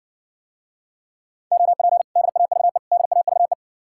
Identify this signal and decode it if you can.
Morse code, heard as a keyed tone, about 60 wpm, 700 Hz. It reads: OP DAVE DAVE